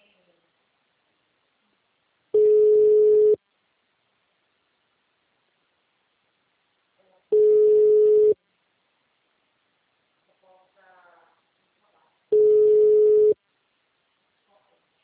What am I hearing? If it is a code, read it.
Ringback tone (Europe, 425 Hz), with speech.
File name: out- -2007- - - .4751